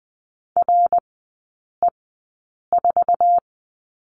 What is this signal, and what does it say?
Morse code, heard as a keyed tone, about 20 wpm, spaced as for 8 wpm, 700 Hz. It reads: RE4
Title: 053 Lesson 10e - Letter r - Three Characters 20wpm